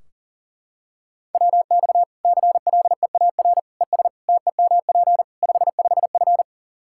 Morse code, heard as a keyed tone, about 40 wpm, 700 Hz. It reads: WX CLEAR ES TEMP 55F
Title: Practice QSO - Exchange 2 - Info - 40wpm